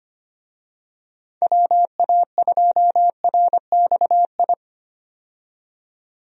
Morse code, heard as a keyed tone, about 25 wpm, 700 Hz. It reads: WA2RXI
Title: USA Call Signs - 6 Characters - 25wpm